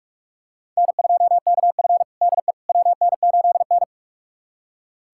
Morse code, heard as a keyed tone, about 46 wpm, 700 Hz. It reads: N2KF DE WN8N
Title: USA Callsign DE Callsign - 4 Characters - 45wpm